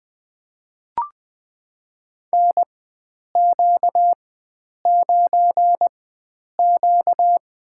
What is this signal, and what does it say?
Morse code, heard as a keyed tone, about 20 wpm, spaced as for 9 wpm, 700 Hz. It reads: NQ9Q
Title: USA Call Signs - 4 Characters - 4x Character Spacing - 20wpm